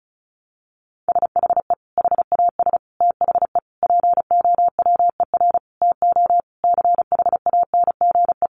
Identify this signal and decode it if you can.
Morse code, heard as a keyed tone, about 34 wpm, 700 Hz. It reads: SHE HAS THE POWER TO CHANGE
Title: Sentences from Top 1200 Words 35wpm